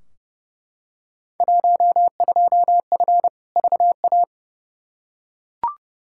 Morse code, heard as a keyed tone, about 29 wpm, 700 Hz. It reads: 12F VA
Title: Field Day Exchanges Encore - 30wpm